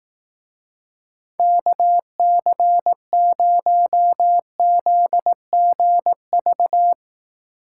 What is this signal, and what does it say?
Morse code, heard as a keyed tone, about 18 wpm, 700 Hz. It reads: KC0ZGV
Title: Speed Racing - USA 2x3 Call Signs 15wpm